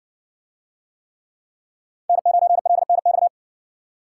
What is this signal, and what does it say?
Morse code, heard as a keyed tone, about 60 wpm, 700 Hz. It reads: N9LNX